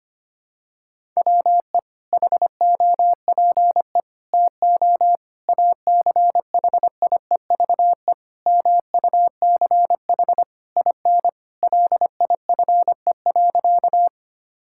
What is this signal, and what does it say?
Morse code, heard as a keyed tone, about 25 wpm, 700 Hz. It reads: WE HOPE TO ACHIEVE MUCH IN LIFE.
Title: Sentences from Top 1000 Words - Review Part 6 - 25wpm